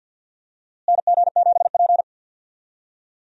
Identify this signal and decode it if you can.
Morse code, heard as a keyed tone, about 52 wpm, 700 Hz. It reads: NG7P
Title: USA Call Signs - 4 Characters - Repeated 2x - 50wpm